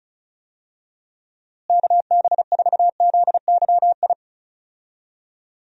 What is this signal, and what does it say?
Morse code, heard as a keyed tone, about 34 wpm, 700 Hz. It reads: KB4ZYI
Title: USA Call Signs - 6 Characters - Repeated 2x - 35wpm